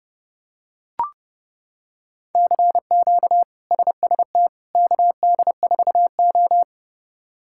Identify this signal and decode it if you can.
Morse code, heard as a keyed tone, about 30 wpm, 700 Hz. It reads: CQ SST KD4O
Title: K1USN SST QSO Practice - 30wpm